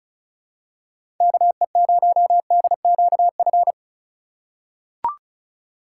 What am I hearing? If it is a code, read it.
Morse code, heard as a keyed tone, about 34 wpm, 700 Hz. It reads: KE0DQF
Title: USA Call Signs - 6 Characters - 4x Character Spacing - 35wpm